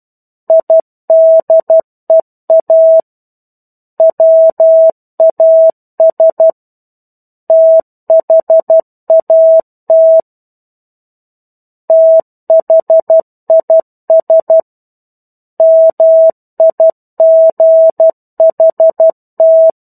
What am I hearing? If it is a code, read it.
Morse code, heard as a keyed tone, about 12 wpm, 650 Hz. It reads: IDEA WAS THAT THIS MIGHT
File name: 01-Chapter-01 12 WPM